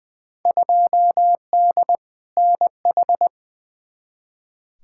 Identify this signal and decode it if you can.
Morse code, heard as a keyed tone, about 21 wpm, 700 Hz. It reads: 2D NH
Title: Field Day Exchanges Encore - 20wpm